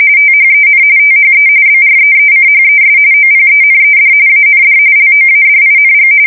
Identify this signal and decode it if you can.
RTTY, about 45.45 baud. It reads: THEN THE BREECHES OPENED AND LOADING
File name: fsk170 / USS-New-Jersey